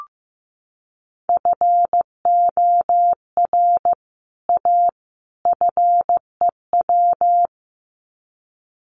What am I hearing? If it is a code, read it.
Morse code, heard as a keyed tone, about 15 wpm, 700 Hz. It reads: FOR A FEW
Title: Sets of 3 Words 15wpm